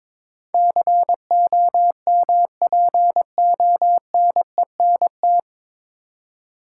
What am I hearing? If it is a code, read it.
Morse code, heard as a keyed tone, about 22 wpm, 700 Hz. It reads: COMPONENT
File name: 9 Letter Words - First 3 Letters Given - 22wpm